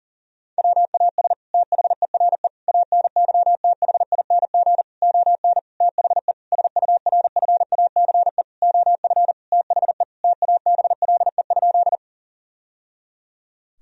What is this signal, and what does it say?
Morse code, heard as a keyed tone, about 40 wpm, 700 Hz. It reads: WAS THERE ANYTHING ON THE SURFACE OF THE TABLE?
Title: Sentences from Top 1000 Words - Review Part 3 - 40wpm